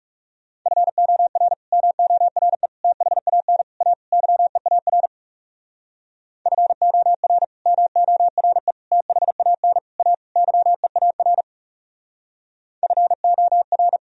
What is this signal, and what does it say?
Morse code, heard as a keyed tone, about 37 wpm, 700 Hz. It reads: FOR MORE THAN A YEAR FOR MORE THAN A YEAR FOR
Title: Speed-Racing - Sets of 5 Words 30wpm